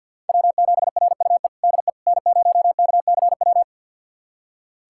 Morse code, heard as a keyed tone, about 48 wpm, 700 Hz. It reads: W7RUE DE N0KCW
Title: USA Callsign DE Callsign - 5 Characters - 50wpm